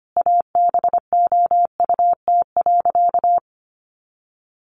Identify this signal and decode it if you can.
Morse code, heard as a keyed tone, about 25 wpm, 700 Hz. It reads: ABOUT.
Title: Sentences from Top 400 Words 25wpm